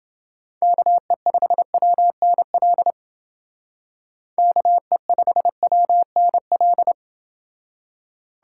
Morse code, heard as a keyed tone, about 29 wpm, 700 Hz. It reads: KE5WNL KE5WNL